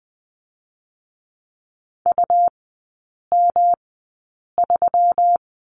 Morse code, heard as a keyed tone, about 20 wpm, spaced as for 8 wpm, 700 Hz. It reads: UM3
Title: 127 Lesson 21e - Number 6 - Three Characters 20wpm